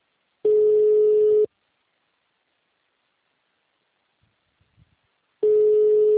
Ringback tone (Europe, 425 Hz).
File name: out- -2011- - - .40100